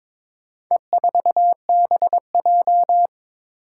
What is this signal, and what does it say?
Morse code, heard as a keyed tone, about 22 wpm, 700 Hz. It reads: E4BJ
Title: Four Challenging Letter-Number Combinations 22wpm